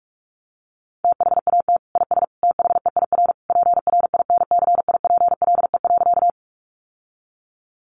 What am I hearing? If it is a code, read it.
Morse code, heard as a keyed tone, about 44 wpm, 700 Hz. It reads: THAT IS THEIR PRINCIPLE.
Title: Sentences from Top 1400 Words 45wpm